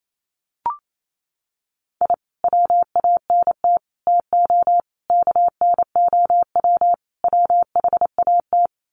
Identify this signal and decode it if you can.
Morse code, heard as a keyed tone, about 28 wpm, 700 Hz. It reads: I WANT TO KNOW WHAT
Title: Sets of 5 Words Encore2 28wpm